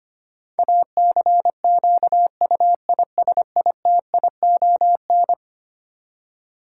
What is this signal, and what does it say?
Morse code, heard as a keyed tone, about 25 wpm, 700 Hz. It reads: ACQUISITION
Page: morse-code-ninja-practice-25wpm